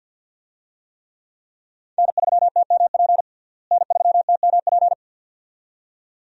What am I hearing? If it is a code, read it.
Morse code, heard as a keyed tone, about 46 wpm, 700 Hz. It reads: N3TMP N3TMP
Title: USA Call Signs - 5 Characters - Repeated 2x - 50wpm